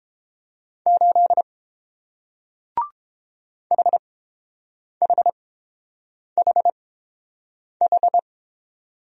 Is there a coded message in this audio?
Yes: Morse code from 0.9 to 8.2 s, heard as a keyed tone, about 30 wpm, spaced as for 7 wpm, 700 Hz. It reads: 8 HHHH